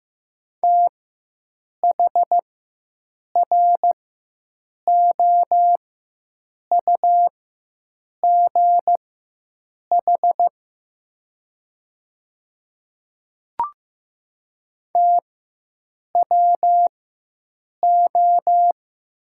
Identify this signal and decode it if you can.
Morse code, heard as a keyed tone, about 15 wpm, spaced as for 7 wpm, 700 Hz. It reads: THROUGH TWO
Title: 4x Character Spacing - Top 200 Words - 15wpm